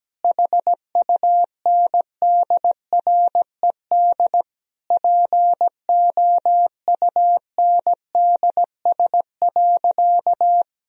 Morse code, heard as a keyed tone, about 17 wpm, 700 Hz. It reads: HUNDRED POUNDS.
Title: Sentences from Top 1000 Words - Review Part 2 - 17wpm